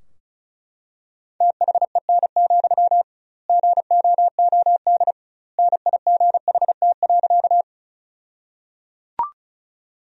Morse code, heard as a keyed tone, about 36 wpm, 700 Hz. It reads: THEN, GOOD NIGHT.
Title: 2x Word-Spacing - Sentences from Top 300 Words - 35wpm